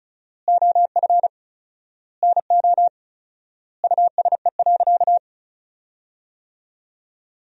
Morse code, heard as a keyed tone, about 36 wpm, 700 Hz. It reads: OF NO USE.
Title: 4x Word-Spacing - Sentences from Top 100 Words - 35wpm